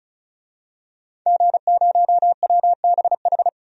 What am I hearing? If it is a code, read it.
Morse code, heard as a keyed tone, about 36 wpm, 700 Hz. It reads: G0WBH
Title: DXCC-100 Call Signs - 5 Characters - 35wpm